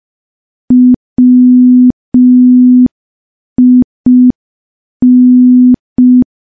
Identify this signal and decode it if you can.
Morse code, heard as a keyed tone, about 5 wpm, 250 Hz. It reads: WIN
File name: morse / morse-training-05WPM